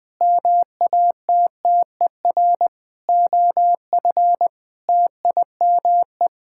Morse code, heard as a keyed tone, about 20 wpm, 700 Hz. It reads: MATTER OF TIME
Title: Sentences with 7-Word Phrases - 20wpm